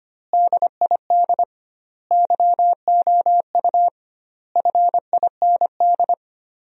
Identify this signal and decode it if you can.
Morse code, heard as a keyed tone, about 25 wpm, 700 Hz. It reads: DID YOU FIND
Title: 2x Word-Spacing - Sentences from Top 200 Words - 25wpm